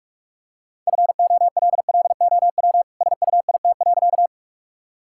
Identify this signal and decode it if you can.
Morse code, heard as a keyed tone, about 41 wpm, 700 Hz. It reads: FOLLOW SUIT.
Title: Sentences from Top 1600 Words 45wpm